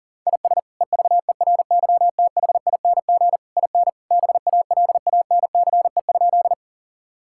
Morse code, heard as a keyed tone, about 37 wpm, 700 Hz. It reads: IS EVERYTHING IN BALANCE?
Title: Sentences from Top 1000 Words - Review Part 6 - 40wpm